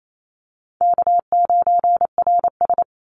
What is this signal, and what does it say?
Morse code, heard as a keyed tone, about 28 wpm, 700 Hz. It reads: K9RS
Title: Easy Call Signs Repeated 3x - 28wpm